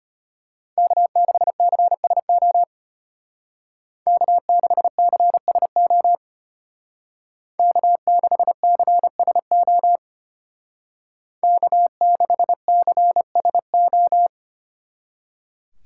Morse code, heard as a keyed tone, about 32 wpm, 700 Hz. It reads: K6CSO K6CSO K6CSO K6CSO